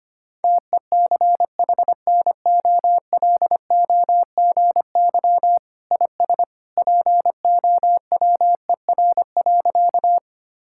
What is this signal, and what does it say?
Morse code, heard as a keyed tone, about 25 wpm, 700 Hz. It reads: TECHNOLOGY IS POWER.
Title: Sentences from Top 1000 Words - Review Part 5 - 25wpm